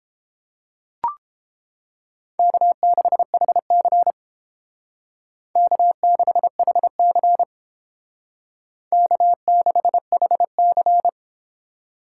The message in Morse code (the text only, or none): K6HC K6HC K6HC